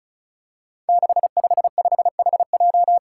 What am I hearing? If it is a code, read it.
Morse code, heard as a keyed tone, about 36 wpm, 700 Hz. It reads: 655HJ
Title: Five Challenging Letter-Number Combinations 35wpm